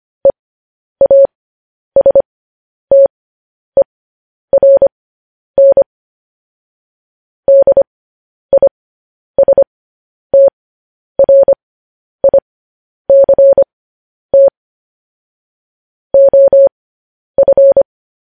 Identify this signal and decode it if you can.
Morse code, heard as a keyed tone, about 25 wpm, spaced as for 10 wpm, 550 Hz. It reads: EASTERN DISTRICT OF
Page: news-headlines-in-morse-code-at-10-wpm